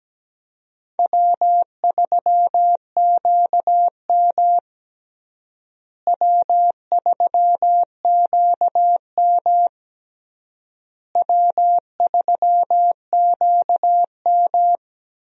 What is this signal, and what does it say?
Morse code, heard as a keyed tone, about 17 wpm, 700 Hz. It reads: W3QM W3QM W3QM
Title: USA Call Signs - 4 Characters - Repeated 3x - 17wpm